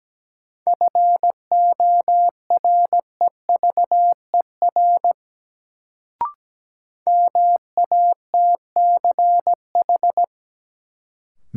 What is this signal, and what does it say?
Morse code, heard as a keyed tone, about 17 wpm, 700 Hz. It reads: FOREVER MATCH
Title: Sentences from Top 1600 Words - New Word Drill - 17wpm